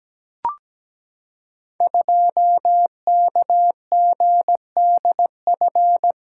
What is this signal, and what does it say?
Morse code, heard as a keyed tone, about 17 wpm, 700 Hz. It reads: 2KGDF
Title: Code Groups - 5 Letter:Numbers - 17wpm